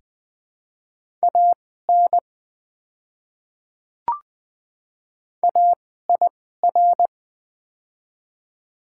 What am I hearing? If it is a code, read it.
Morse code, heard as a keyed tone, about 20 wpm, spaced as for 14 wpm, 700 Hz. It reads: AN AIR
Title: 2x Character Spacing - Top 200 Words - 20wpm